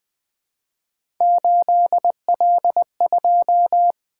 Morse code, heard as a keyed tone, about 20 wpm, 700 Hz. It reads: 8L2
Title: Three Letter-Number Combinations - 20wpm